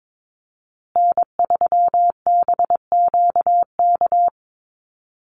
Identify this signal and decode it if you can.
Morse code, heard as a keyed tone, about 22 wpm, 700 Hz. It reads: N3BQK
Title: USA Call Signs Repeated 2x 22wpm